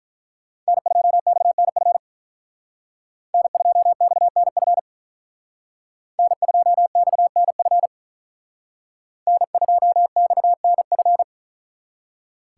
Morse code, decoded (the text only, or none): N2XNF N2XNF N2XNF N2XNF